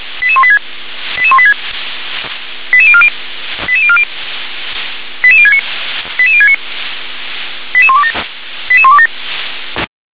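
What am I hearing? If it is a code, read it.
Five-tone selective call (ZVEI), heard as quick runs of five tones, tones of about 70 ms. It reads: 88186 88186 88040 88040 88069 88069 88127 88127
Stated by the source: ZVEI five-tone selective call received on SDR.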